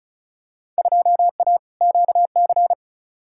Morse code, heard as a keyed tone, about 36 wpm, 700 Hz. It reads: 2A QC
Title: Field Day Exchanges - 35wpm